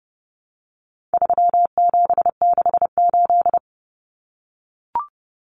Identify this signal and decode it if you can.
Morse code, heard as a keyed tone, about 30 wpm, 700 Hz. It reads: 3768
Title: Numbers - 4-Digit 30wpm